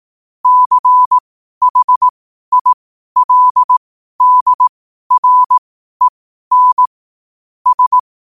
Morse code, heard as a keyed tone, about 18 wpm, spaced as for 12 wpm, 1000 Hz. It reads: CHILDREN S
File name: qotdfiles / 12wpm